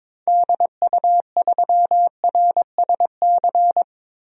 Morse code, heard as a keyed tone, about 21 wpm, 700 Hz. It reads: DU3RSC
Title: DXCC-100 Call Signs - 6 Characters - 22wpm